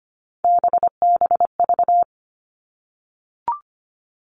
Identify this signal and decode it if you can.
Morse code, heard as a keyed tone, about 25 wpm, 700 Hz. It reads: BBV
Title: All Cut Numbers - 3-Digit 25wpm